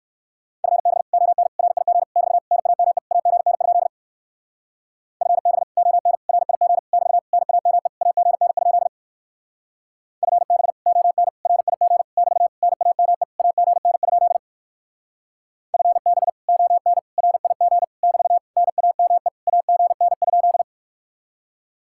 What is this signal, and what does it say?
Morse code, heard as a keyed tone, about 62 wpm, 700 Hz. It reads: FB ON RIG <BT> NAME AGN? FB ON RIG <BT> NAME AGN? FB ON RIG <BT> NAME AGN? FB ON RIG <BT> NAME AGN?